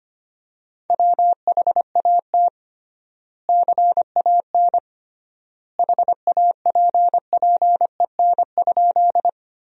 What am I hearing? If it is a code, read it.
Morse code, heard as a keyed tone, about 25 wpm, 700 Hz. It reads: WHAT CAN HAPPEN?
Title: 3x Word-Spacing - Sentences from Top 400 Words - 25wpm